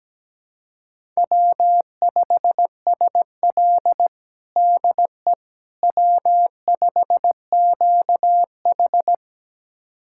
Morse code, heard as a keyed tone, about 17 wpm, 700 Hz. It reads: W5SL DE W5QH
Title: USA Callsign DE Callsign - 4 Characters - 17wpm